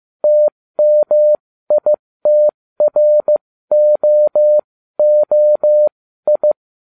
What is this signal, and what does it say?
Morse code, heard as a keyed tone, about 15 wpm, spaced as for 13 wpm, 600 Hz. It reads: TMITROOI